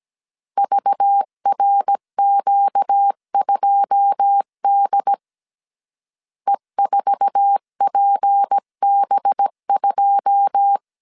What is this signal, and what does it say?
Morse code, heard as a keyed tone, about 17 wpm, 800 Hz. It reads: VRQ2D E4PB2
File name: 1500C S B 16wpm